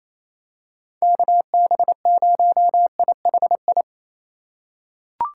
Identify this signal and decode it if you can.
Morse code, heard as a keyed tone, about 28 wpm, 700 Hz. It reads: KB0IHI
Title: USA Call Signs - 6 Characters - 2x Character Spacing - 28wpm